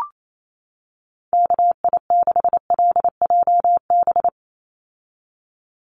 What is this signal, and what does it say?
Morse code, heard as a keyed tone, about 28 wpm, 700 Hz. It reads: KI6LJB
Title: USA Call Signs 28wpm